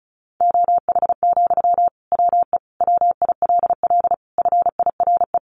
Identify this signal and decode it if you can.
Morse code, heard as a keyed tone, about 34 wpm, 700 Hz. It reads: OH, WE WILL FIRE
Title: Sentences from Top 1100 Words 35wpm